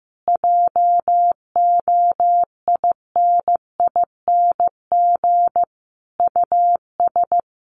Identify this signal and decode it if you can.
Morse code, heard as a keyed tone, about 15 wpm, 700 Hz. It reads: JOINING US
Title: Sets of 5 Words 15wpm